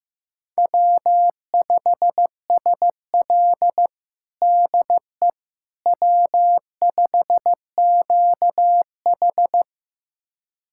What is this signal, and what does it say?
Morse code, heard as a keyed tone, about 15 wpm, 700 Hz. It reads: W5SL DE W5QH
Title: USA Callsign DE Callsign - 4 Characters - 15wpm